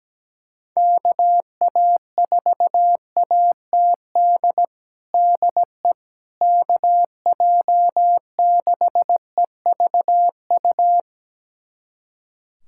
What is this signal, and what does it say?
Morse code, heard as a keyed tone, about 17 wpm, 700 Hz. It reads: KA4ATD DE KJ6EVU
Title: USA Callsign DE Callsign - 6 Characters - 17wpm